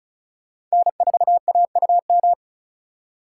Morse code, heard as a keyed tone, about 36 wpm, 690 Hz. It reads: N4AUM
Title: USA Call Signs - 5 Characters - 3x Character Spacing - 35wpm